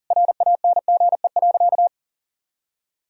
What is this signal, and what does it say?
Morse code, heard as a keyed tone, about 40 wpm, 700 Hz. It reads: RANGE.
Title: Sentences from Top 1000 Words - Review Part 4 - 40wpm